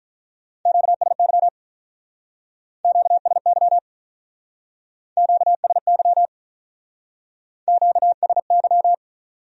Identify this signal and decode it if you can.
Morse code, heard as a keyed tone, about 45 wpm, 700 Hz. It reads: QSY QSY QSY QSY